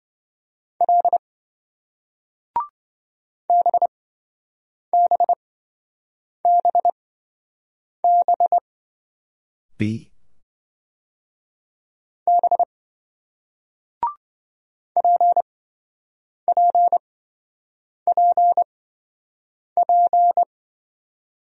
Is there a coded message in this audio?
Yes: Morse code from 0.8 to 20.4 s, heard as a keyed tone, about 27 wpm, spaced as for 7 wpm, 700 Hz. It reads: L BBBB B PPPP